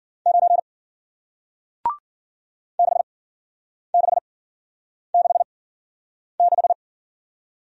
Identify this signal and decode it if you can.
Morse code, heard as a keyed tone, about 48 wpm, spaced as for 9 wpm, 700 Hz. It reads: 9 6666
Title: Speed-Racing Single Letter-Number 40wpm